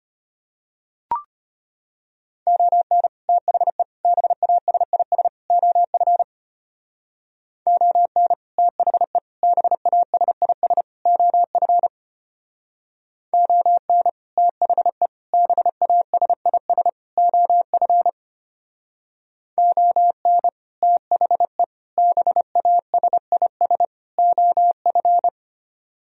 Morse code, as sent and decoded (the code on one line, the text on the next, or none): --- -. / - .... / -... .- ... .. ... / --- ..-. / --- -. / - .... / -... .- ... .. ... / --- ..-. / --- -. / - .... / -... .- ... .. ... / --- ..-. / --- -. / - .... / -... .- ... .. ... / --- ..-.
ON THE BASIS OF ON THE BASIS OF ON THE BASIS OF ON THE BASIS OF